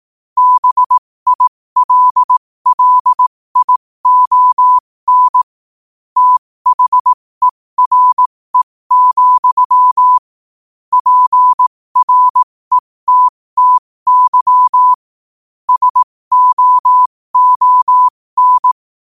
Morse code, heard as a keyed tone, about 18 wpm, spaced as for 15 wpm, 1000 Hz. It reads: BILLION THERE, PRETTY SOON
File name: qotdfiles / 15wpm